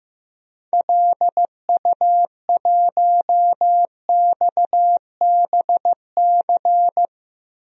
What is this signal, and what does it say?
Morse code, heard as a keyed tone, about 15 wpm, 700 Hz. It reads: LU1XBC
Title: DXCC-100 Call Signs - 6 Characters - Repeated 3x - 15wpm